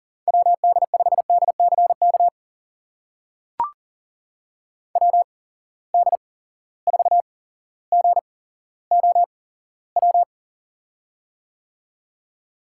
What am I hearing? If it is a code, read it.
Morse code, heard as a keyed tone, about 39 wpm, spaced as for 17 wpm, 700 Hz. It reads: WD5DCK WD4GOW
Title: USA Call Signs - 6 Characters - 8x Character Spacing - 40wpm